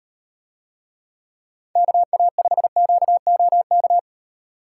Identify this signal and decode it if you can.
Morse code, heard as a keyed tone, about 35 wpm, 700 Hz. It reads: KA5QOK